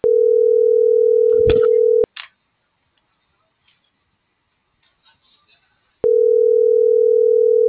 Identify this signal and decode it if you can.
Ringback tone (North America), 440 + 480 Hz.